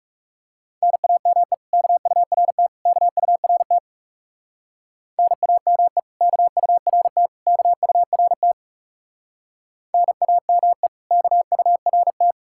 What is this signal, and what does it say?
Morse code, heard as a keyed tone, about 40 wpm, 700 Hz. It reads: NAME KURT KURT NAME KURT KURT NAME KURT